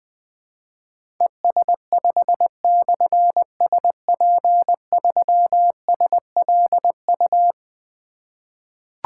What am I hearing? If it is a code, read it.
Morse code, heard as a keyed tone, about 20 wpm, 700 Hz. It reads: ES5/SP3SLU
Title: Difficult Call Signs 20wpm